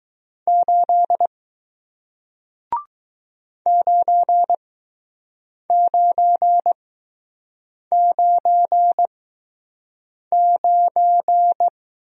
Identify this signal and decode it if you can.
Morse code, heard as a keyed tone, about 20 wpm, 700 Hz. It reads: 8 9 9 9 9